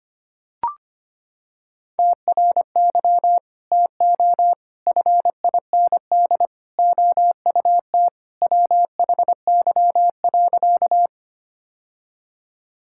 Morse code, heard as a keyed tone, about 25 wpm, 700 Hz. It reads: TRY TO FIND OUT WHY.
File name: practice / 25wpm / Sentences with 3-Word Phrases - 25wpm